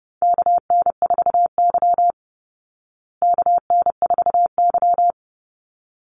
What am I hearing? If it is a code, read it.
Morse code, heard as a keyed tone, about 30 wpm, 700 Hz. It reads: KN4Y KN4Y
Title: Call Signs Repeated 3x - 30wpm